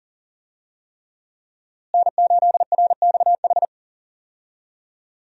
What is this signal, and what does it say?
Morse code, heard as a keyed tone, about 38 wpm, 700 Hz. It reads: N8RXH
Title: USA Call Signs - 5 Characters - Repeated 3x - 40wpm